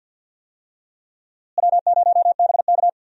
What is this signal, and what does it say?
Morse code, heard as a keyed tone, about 51 wpm, 700 Hz. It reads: W0BK
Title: USA Call Signs - 4 Characters - Repeated 2x - 50wpm